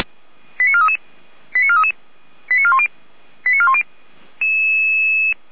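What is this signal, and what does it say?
Five-tone selective call (ZVEI), heard as quick runs of five tones, tones of about 70 ms. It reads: 89433 89433 89410 89410
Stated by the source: ZVEI five-tone selective call 88401 received by SDR.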